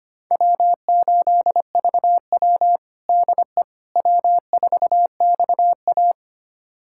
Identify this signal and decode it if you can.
Morse code, heard as a keyed tone, about 25 wpm, 700 Hz. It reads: W8VW DE W4XA
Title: USA Callsign DE Callsign - 4 Characters - 25wpm